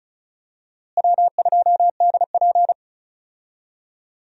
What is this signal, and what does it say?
Morse code, heard as a keyed tone, about 36 wpm, 700 Hz. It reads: W2DP
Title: USA Call Signs - 4 Characters - Repeated 3x - 35wpm